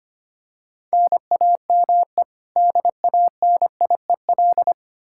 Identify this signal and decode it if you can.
Morse code, heard as a keyed tone, about 25 wpm, 700 Hz. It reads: NAME DANIEL
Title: Practice QSO - Exchange 1 - Beginning - 25wpm